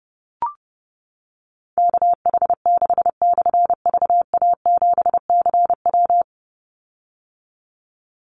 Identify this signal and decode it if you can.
Morse code, heard as a keyed tone, about 30 wpm, 700 Hz. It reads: KH6/VA7CW
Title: Call Signs - 30wpm